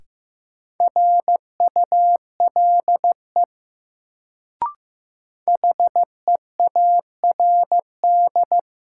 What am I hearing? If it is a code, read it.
Morse code, heard as a keyed tone, about 15 wpm, 700 Hz. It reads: RULE HEARD
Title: Sentences from Top 400 Words - New Word Drill - 15wpm